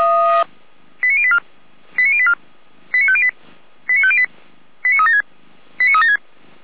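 Five-tone selective call (ZVEI), heard as quick runs of five tones, tones of about 70 ms. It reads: 89084 89084 89508 89508 89376 89376
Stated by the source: Five-tone ZVEI call received on an SDR.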